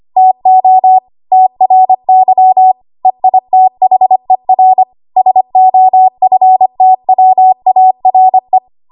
Morse code, heard as a keyed tone, about 25 wpm, 750 Hz. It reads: TO TRY EITHER SOFTWARE
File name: Archive / 25 WPM / 25WPM